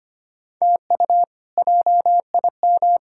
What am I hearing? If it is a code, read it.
Morse code, heard as a keyed tone, about 25 wpm, 700 Hz. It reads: TU JIM
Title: K1USN SST QSO Practice - 25wpm